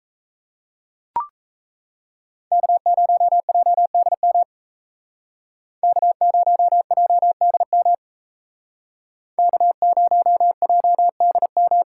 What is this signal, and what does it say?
Morse code, heard as a keyed tone, about 35 wpm, 700 Hz. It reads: K0JDM K0JDM K0JDM
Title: Speed Racing - USA 1x3 Call Signs 28wpm